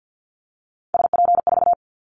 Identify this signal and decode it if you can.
Morse code, heard as a keyed tone, about 50 wpm, 700 Hz. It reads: SP4
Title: Callsign Prefixes for Top 25 DXCC - 50wpm